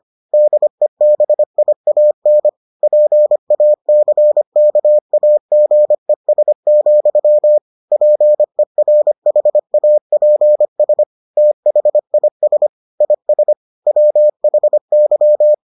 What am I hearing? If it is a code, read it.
Morse code, heard as a keyed tone, about 25 wpm, 600 Hz. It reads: DEBIAN PACKAGES, PERHAPS THIS IS WHY